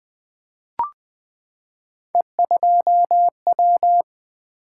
Morse code, heard as a keyed tone, about 20 wpm, 700 Hz. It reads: E2W
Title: Three Letter-Number Combinations - 20wpm